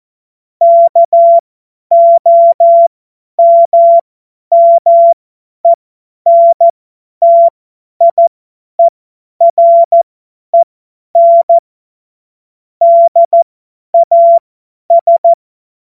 Morse code, heard as a keyed tone, about 14 wpm, spaced as for 10 wpm, 680 Hz. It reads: KOMMENTIEREN DAS
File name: Morsekurs / Witze / Witz 26. April 2022 50 F70